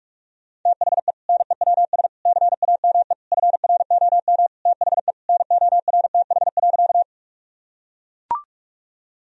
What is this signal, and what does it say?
Morse code, heard as a keyed tone, about 44 wpm, 700 Hz. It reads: THE NEWS CAME FROM THE NORTH.
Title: Sentences from Top 1000 Words - Review Part 1 - 45wpm